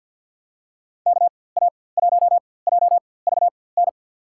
Morse code, heard as a keyed tone, about 50 wpm, spaced as for 23 wpm, 700 Hz. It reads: KA1JVN
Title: USA Call Signs - 6 Characters - 4x Character Spacing - 50wpm